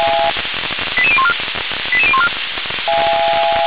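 Five-tone selective call (ZVEI), heard as quick runs of five tones, tones of about 70 ms. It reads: 88014 88014
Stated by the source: Received on an SDR, ZVEI five-tone call.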